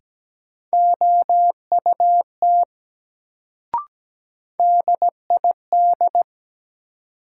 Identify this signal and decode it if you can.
Morse code, heard as a keyed tone, about 17 wpm, 700 Hz. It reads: OUT DID
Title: Sentences from Top 100 Words - New Word Drill - 17wpm